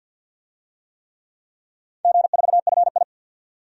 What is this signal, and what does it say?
Morse code, heard as a keyed tone, about 47 wpm, 700 Hz. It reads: G4FI
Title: DXCC-100 Call Signs - 4 Characters - 50wpm